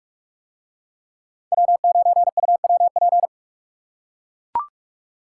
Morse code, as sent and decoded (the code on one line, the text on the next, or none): .-- ----. ..- .-- .--.
W9UWP